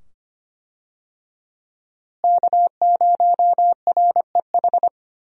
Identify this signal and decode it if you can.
Morse code, heard as a keyed tone, about 25 wpm, 700 Hz. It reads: K0REH
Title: USA Call Signs - 5 Characters - Repeated 2x - 25wpm